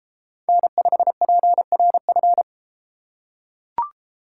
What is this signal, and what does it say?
Morse code, heard as a keyed tone, about 33 wpm, 700 Hz. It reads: N5PRF